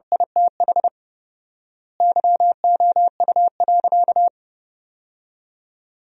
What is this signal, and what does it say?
Morse code, heard as a keyed tone, about 30 wpm, 700 Hz. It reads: ITH YOU.
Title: 4x Word-Spacing - Sentences from Top 300 Words - 30wpm